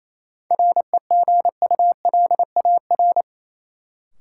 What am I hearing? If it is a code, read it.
Morse code, heard as a keyed tone, about 28 wpm, 700 Hz. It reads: REGULAR
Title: Sentences from Top 900 Words - New Word Drill - 28wpm